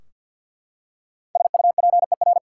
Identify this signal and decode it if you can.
Morse code, heard as a keyed tone, about 52 wpm, 700 Hz. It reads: SUPER
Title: Five Letter Words - 50wpm